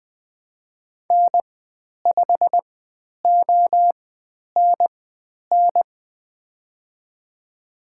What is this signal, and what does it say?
Morse code, heard as a keyed tone, about 20 wpm, spaced as for 10 wpm, 700 Hz. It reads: N5ONN
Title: 095 Lesson 16g - Letter c - Call Signs 20wpm